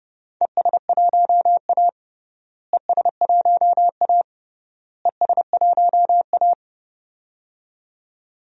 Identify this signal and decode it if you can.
Morse code, heard as a keyed tone, about 30 wpm, 700 Hz. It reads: ES1A ES1A ES1A
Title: DXCC-100 Call Signs - 4 Characters - Repeated 3x - 30wpm